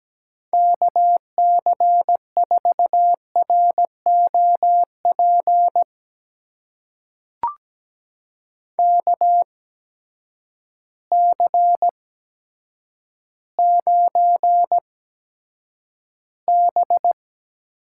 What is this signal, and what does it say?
Morse code, heard as a keyed tone, about 17 wpm, spaced as for 8 wpm, 700 Hz. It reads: KC4ROP KC9B
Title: USA Call Signs - 6 Characters - 8x Character Spacing - 17wpm